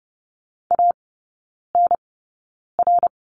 Morse code, heard as a keyed tone, about 30 wpm, spaced as for 9 wpm, 700 Hz. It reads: ANR
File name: Lesson 16e - Letter c - Three Characters 30wpm